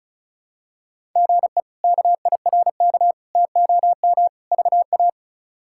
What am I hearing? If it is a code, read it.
Morse code, heard as a keyed tone, about 36 wpm, 700 Hz. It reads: GE KIRK TOM VA
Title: K1USN SST QSO Practice - 35wpm